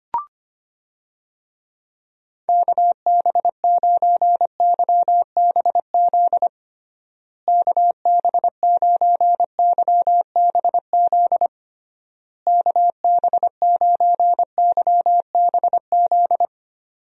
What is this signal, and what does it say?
Morse code, heard as a keyed tone, about 25 wpm, 700 Hz. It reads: KB9YBZ KB9YBZ KB9YBZ
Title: USA Call Signs - 6 Characters - Repeated 3x - 25wpm